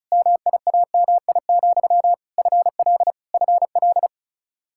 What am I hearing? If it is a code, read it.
Morse code, heard as a keyed tone, about 36 wpm, 700 Hz. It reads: MIAMI, FL FL
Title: Practice QSO - Exchange 1 - Beginning - 35wpm